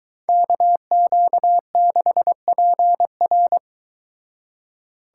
Morse code, heard as a keyed tone, about 22 wpm, 700 Hz. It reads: KQ6PR